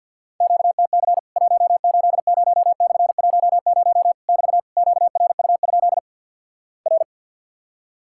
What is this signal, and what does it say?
Morse code, heard as a keyed tone, about 50 wpm, 700 Hz. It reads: QTC 180/10 <BT> QRU? R
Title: Worked All Europe DX Contest - QTC Exchange - 50wpm